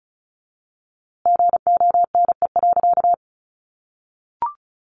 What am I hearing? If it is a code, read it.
Morse code, heard as a keyed tone, about 34 wpm, 700 Hz. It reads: GONE.
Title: A Sound Of Thunder 35wpm